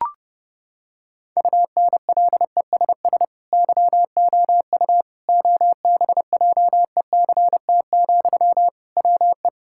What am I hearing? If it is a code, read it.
Morse code, heard as a keyed tone, about 30 wpm, 700 Hz. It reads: UNLESS YOU OBJECT, WE
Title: Sentences from Top 1000 Words - Review Part 6 - 30wpm